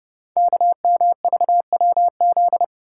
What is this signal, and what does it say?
Morse code, heard as a keyed tone, about 30 wpm, 700 Hz. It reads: KMVWZ
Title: Code Groups - 5 Letters - 30wpm